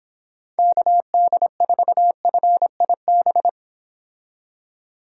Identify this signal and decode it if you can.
Morse code, heard as a keyed tone, about 26 wpm, 700 Hz. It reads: KD4FIB